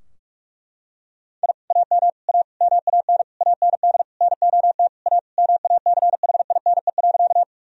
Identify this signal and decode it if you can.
Morse code, heard as a keyed tone, about 43 wpm, 700 Hz. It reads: I AM A MAN AND NOT A MACHINE.
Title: Sentences from Top 1000 Words - Review Part 2 - 45wpm